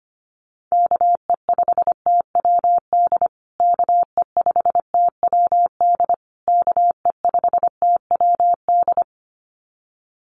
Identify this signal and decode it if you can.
Morse code, heard as a keyed tone, about 25 wpm, 700 Hz. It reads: KE5TWD KE5TWD KE5TWD
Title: USA Call Signs Repeated 3x 25wpm